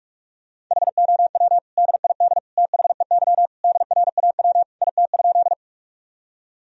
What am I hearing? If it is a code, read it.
Morse code, heard as a keyed tone, about 46 wpm, 700 Hz. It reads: HOW DID THEY DRAW IT?
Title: Sentences from Top 1000 Words - Review Part 1 - 45wpm